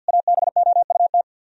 Morse code, heard as a keyed tone, about 64 wpm, 700 Hz. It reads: ABOUT